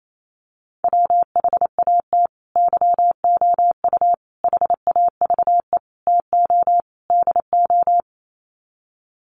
Morse code, heard as a keyed tone, about 28 wpm, 700 Hz. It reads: WHAT YOU HAVE TO DO
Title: Sets of 5 Words Encore 28wpm